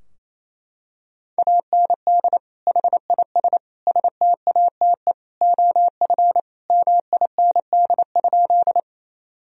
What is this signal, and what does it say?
Morse code, heard as a keyed tone, about 28 wpm, 700 Hz. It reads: AND HIS STATE OF MIND?
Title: Sentences from Top 1000 Words - Review Part 2 - 28wpm